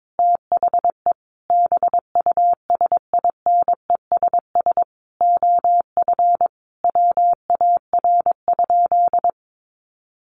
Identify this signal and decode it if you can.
Morse code, heard as a keyed tone, about 22 wpm, 700 Hz. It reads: THE BUSINESS OF WAR?
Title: Sentences from Top 1300 Words 22wpm